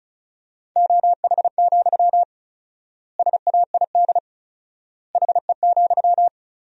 Morse code, heard as a keyed tone, about 34 wpm, 700 Hz. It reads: OH, SAID HE,
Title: 4x Word-Spacing - Sentences from Top 500 Words - 35wpm